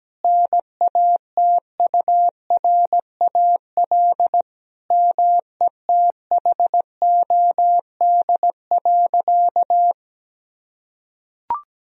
Morse code, heard as a keyed tone, about 17 wpm, 700 Hz. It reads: NATURAL METHOD.
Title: Sentences from Top 1000 Words - Review Part 4 - 17wpm